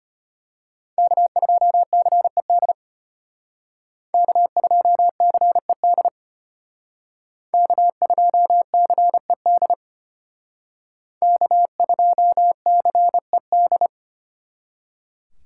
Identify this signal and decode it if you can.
Morse code, heard as a keyed tone, about 31 wpm, 700 Hz. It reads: K2CED K2CED K2CED K2CED
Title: Speed Racing - USA Call Signs 25wpm